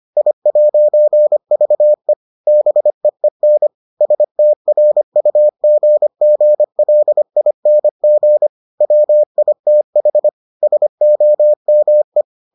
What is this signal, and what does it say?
Morse code, heard as a keyed tone, about 25 wpm, 600 Hz. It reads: I'VE BEEN STRUGGLING WITH SOME